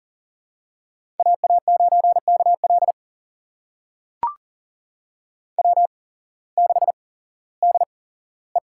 Morse code, heard as a keyed tone, about 41 wpm, spaced as for 19 wpm, 700 Hz. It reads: AA9KL W6DE
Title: USA Call Signs - 5 Characters - 8x Character Spacing - 40wpm